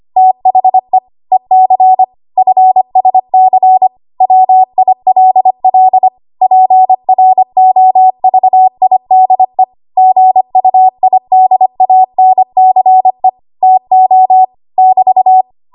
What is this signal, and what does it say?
Morse code, heard as a keyed tone, about 25 wpm, 750 Hz. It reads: THE EC FSC WILL PROVIDE GUIDANCE TO <BT>
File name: Archive / 25 WPM / 25WPM